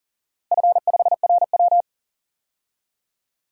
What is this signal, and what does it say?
Morse code, heard as a keyed tone, about 40 wpm, 700 Hz. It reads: F5RW
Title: DXCC-100 Call Signs - 4 Characters - Repeated 3x - 40wpm